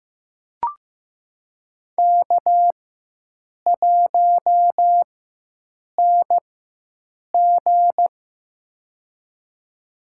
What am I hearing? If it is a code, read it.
Morse code, heard as a keyed tone, about 16 wpm, spaced as for 7 wpm, 700 Hz. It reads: K1NG
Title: USA Call Signs - 4x Character Spacing - 15wpm